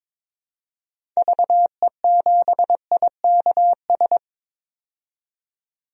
Morse code, heard as a keyed tone, about 22 wpm, 700 Hz. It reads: VE7IKS
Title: DXCC-100 Call Signs - 6 Characters - 22wpm